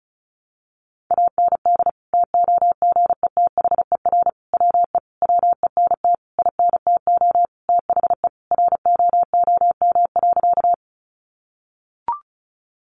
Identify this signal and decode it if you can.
Morse code, heard as a keyed tone, about 34 wpm, 700 Hz. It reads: AND TOGETHER WE WENT INTO THE ROOM.
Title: Sentences from Top 600 Words 35wpm